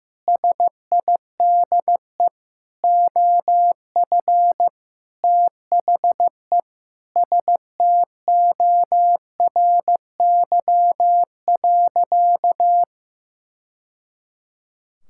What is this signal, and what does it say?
Morse code, heard as a keyed tone, about 15 wpm, 700 Hz. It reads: SIDE OF THE STORY.
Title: Sentences from Top 1000 Words - Review Part 5 - 15wpm